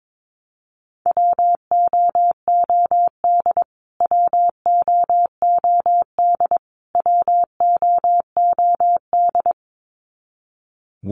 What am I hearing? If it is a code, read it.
Morse code, heard as a keyed tone, about 22 wpm, 700 Hz. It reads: WOOD WOOD WOOD
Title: Top 300 Words Repeated 22wpm